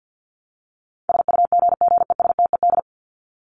Morse code, heard as a keyed tone, about 50 wpm, 700 Hz. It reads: SUGGESTED
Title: ED Suffix - Top 100 Words with Base Words 50wpm